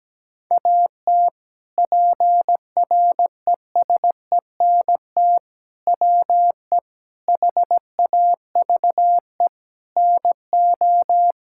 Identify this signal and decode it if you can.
Morse code, heard as a keyed tone, about 17 wpm, 700 Hz. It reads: AT PRESENT WE HAVE NO
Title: Sentences from Top 1000 Words - Review Part 5 - 17wpm